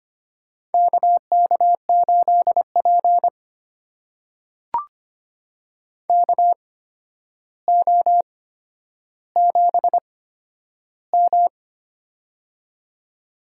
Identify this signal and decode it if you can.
Morse code, heard as a keyed tone, about 25 wpm, spaced as for 11 wpm, 700 Hz. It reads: KK8P KO7M
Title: USA Call Signs - 4 Characters - 8x Character Spacing - 25wpm